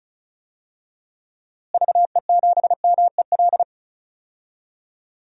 Morse code, heard as a keyed tone, about 34 wpm, 700 Hz. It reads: VE7MEL
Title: DXCC-100 Call Signs - 6 Characters - Repeated 3x - 35wpm